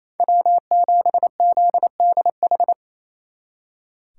Morse code, heard as a keyed tone, about 35 wpm, 700 Hz. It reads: W7ZDH
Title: Speed Racing - USA Call Signs 28wpm